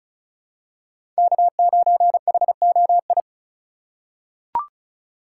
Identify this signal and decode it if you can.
Morse code, heard as a keyed tone, about 36 wpm, 700 Hz. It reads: K9HOI
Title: USA Call Signs - 5 Characters - 4x Character Spacing - 35wpm